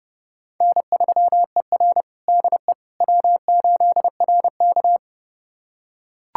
Morse code, heard as a keyed tone, about 30 wpm, 700 Hz. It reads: N3ER DE W8RK
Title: USA Callsign DE Callsign - 4 Characters - 30wpm